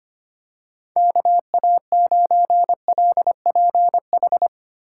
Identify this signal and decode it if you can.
Morse code, heard as a keyed tone, about 25 wpm, 700 Hz. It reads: KA9LPH
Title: USA Call Signs - 6 Characters - 3x Character Spacing - 25wpm